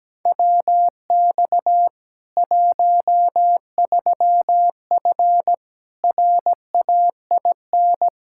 Morse code, heard as a keyed tone, about 17 wpm, 700 Hz. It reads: WX 13F RAIN
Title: K1USN SST QSO Practice 17wpm